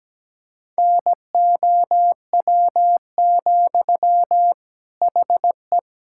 Morse code, heard as a keyed tone, about 17 wpm, 700 Hz. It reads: NOW, HE
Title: Sentences from Top 1600 Words 17wpm